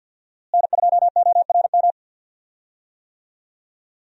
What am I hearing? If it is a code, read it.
Morse code, heard as a keyed tone, about 52 wpm, 700 Hz. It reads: N2ORM
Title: USA Call Signs - 5 Characters - Repeated 3x - 50wpm